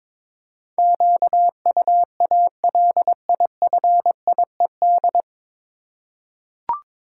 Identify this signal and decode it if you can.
Morse code, heard as a keyed tone, about 22 wpm, 700 Hz. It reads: QUALIFIED
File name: 9 Letter Words - First 3 Letters Given - 22wpm